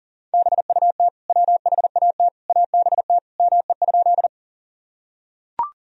Morse code, heard as a keyed tone, about 40 wpm, 700 Hz. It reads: BUT WHAT ABT ME?
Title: Sentences from Top 200 Words and Abbreviations 40wpm